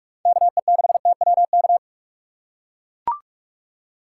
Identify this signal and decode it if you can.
Morse code, heard as a keyed tone, about 44 wpm, 700 Hz. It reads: KE6TWK
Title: USA Call Signs - 6 Characters - 6x Character Spacing - 45wpm